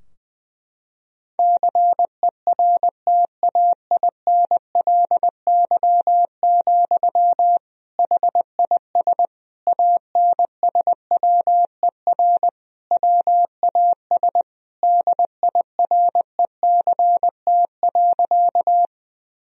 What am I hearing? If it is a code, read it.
Morse code, heard as a keyed tone, about 21 wpm, 700 Hz. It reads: CERTAINLY, HIS ANSWER WAS DIRECT.
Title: Sentences from Top 1000 Words - Review Part 2 - 20wpm